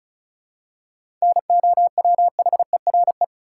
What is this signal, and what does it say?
Morse code, heard as a keyed tone, about 36 wpm, 700 Hz. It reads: NOWHERE
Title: Seven Letter Words - 35wpm